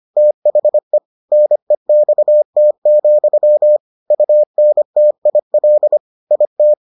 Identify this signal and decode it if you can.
Morse code, heard as a keyed tone, about 25 wpm, 600 Hz. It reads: THE NEXT, UNTIL IT